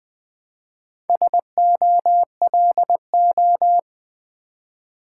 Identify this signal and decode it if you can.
Morse code, heard as a keyed tone, about 20 wpm, 700 Hz. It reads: SOLO
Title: Build A Word - 2 Letters At A Time - 4-Letter Words - 20wpm